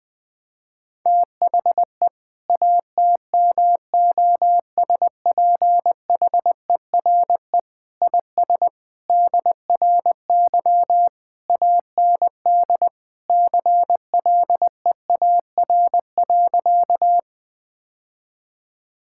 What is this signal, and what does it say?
Morse code, heard as a keyed tone, about 20 wpm, 700 Hz. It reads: THE ATMOSPHERE IS DRY AND CLEAR.
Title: Sentences from Top 1600 Words 20wpm